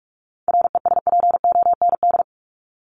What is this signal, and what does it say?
Morse code, heard as a keyed tone, about 44 wpm, 700 Hz. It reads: RESPOND
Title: Top 1751-2000 Words 45wpm